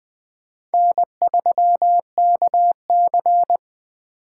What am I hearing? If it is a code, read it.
Morse code, heard as a keyed tone, about 20 wpm, 700 Hz. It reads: N3KC
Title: Four Letter-Number Combinations - 20wpm